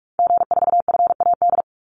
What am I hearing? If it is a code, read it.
Morse code, heard as a keyed tone, about 44 wpm, 700 Hz. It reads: G4FAD
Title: Call Signs Repeated 3x - 45wpm